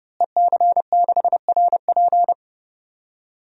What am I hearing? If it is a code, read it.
Morse code, heard as a keyed tone, about 30 wpm, 700 Hz. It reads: EC6RP
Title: DXCC-100 Call Signs - 5 Characters - Repeated 3x - 30wpm